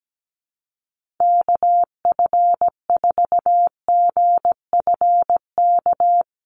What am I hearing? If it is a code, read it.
Morse code, heard as a keyed tone, about 17 wpm, 700 Hz. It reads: KF4GFK
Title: USA Call Signs Repeated 2x 17wpm